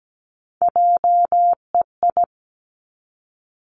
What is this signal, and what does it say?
Morse code, heard as a keyed tone, about 17 wpm, 700 Hz. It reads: JEI
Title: Three Challenging Letter-Number Combinations 17wpm